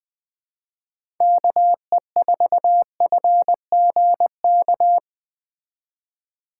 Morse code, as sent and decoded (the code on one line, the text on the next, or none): -.- . ....- ..-. --. -.-
KE4FGK